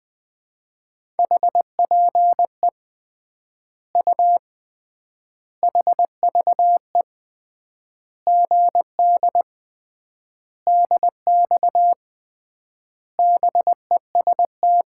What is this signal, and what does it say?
Morse code, heard as a keyed tone, about 20 wpm, 700 Hz. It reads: HPE U HVE GD DX BEST
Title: Common QSO Phrases - 3x Word-Spacing - 20wpm